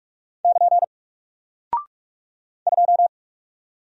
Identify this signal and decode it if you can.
Morse code, heard as a keyed tone, about 44 wpm, 700 Hz. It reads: <KN> 2
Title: Single Character (Letter:Number:Punctuation:Prosign) Alt 45wpm